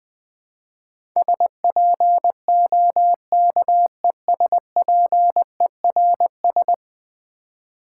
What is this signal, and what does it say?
Morse code, heard as a keyed tone, about 20 wpm, 700 Hz. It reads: SPOKESPERS
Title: Build A Word - 5 Letters At A Time - 12-Letter Words - 20wpm